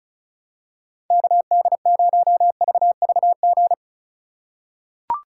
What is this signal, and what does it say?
Morse code, heard as a keyed tone, about 36 wpm, 700 Hz. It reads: KD0VVG
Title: USA Call Signs - 4x Character Spacing - 35wpm